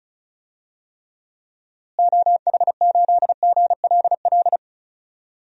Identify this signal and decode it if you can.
Morse code, heard as a keyed tone, about 34 wpm, 700 Hz. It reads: OH8GLL
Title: DXCC-100 Call Signs - 6 Characters - Repeated 3x - 35wpm